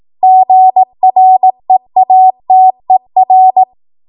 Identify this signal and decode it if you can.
Morse code, heard as a keyed tone, about 18 wpm, 750 Hz. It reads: GREATER
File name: Morse / 18WPM